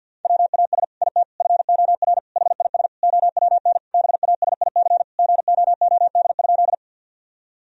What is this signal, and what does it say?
Morse code, heard as a keyed tone, about 50 wpm, 700 Hz. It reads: WAS IT FOR HIS OWN BASIC GOOD?
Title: Sentences from Top 1000 Words - Review Part 3 - 50wpm